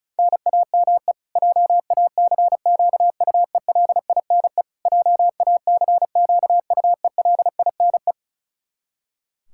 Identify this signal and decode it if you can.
Morse code, heard as a keyed tone, about 34 wpm, 700 Hz. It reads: NAME JACQUELINE JACQUELINE
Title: Practice QSO - Exchange 1 - Beginning - 35wpm